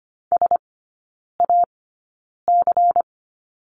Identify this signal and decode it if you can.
Morse code, heard as a keyed tone, about 25 wpm, spaced as for 9 wpm, 700 Hz. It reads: SAC
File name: course / 25:10wpm / 093 Lesson 16e - Letter c - Three Characters 25wpm